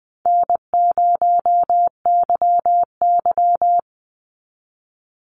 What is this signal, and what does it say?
Morse code, heard as a keyed tone, about 20 wpm, 700 Hz. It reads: N0YY
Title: Call Signs Repeated 2x - 20wpm